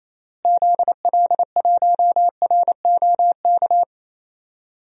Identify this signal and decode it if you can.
Morse code, heard as a keyed tone, about 28 wpm, 700 Hz. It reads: ZL1ROK
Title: DXCC-100 Call Signs - 6 Characters - Repeated 3x - 28wpm